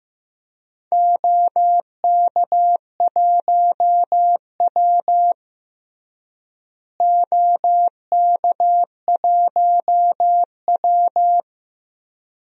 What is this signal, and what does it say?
Morse code, heard as a keyed tone, about 15 wpm, 700 Hz. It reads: OK1W OK1W
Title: DXCC-100 Call Signs - 4 Characters - Repeated 2x - 15wpm